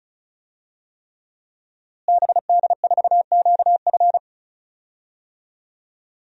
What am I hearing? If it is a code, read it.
Morse code, heard as a keyed tone, about 34 wpm, 700 Hz. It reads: BD4QF
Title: DXCC-100 Call Signs - 5 Characters - Repeated 3x - 35wpm